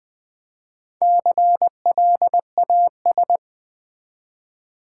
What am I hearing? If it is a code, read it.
Morse code, heard as a keyed tone, about 20 wpm, 700 Hz. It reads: CLAS
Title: Build A Word - 1 Letter At A Time - 5-Letter Words - 20wpm